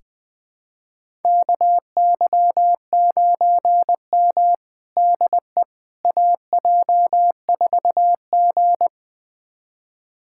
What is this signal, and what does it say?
Morse code, heard as a keyed tone, about 21 wpm, 700 Hz. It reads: KY9M DE AJ4G
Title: USA Callsign DE Callsign - 4 Characters - 20wpm